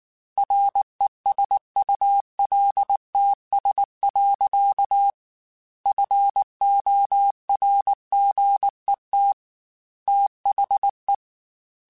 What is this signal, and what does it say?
Morse code, heard as a keyed tone, about 19 wpm, 800 Hz. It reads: RESULTS. FORGET THE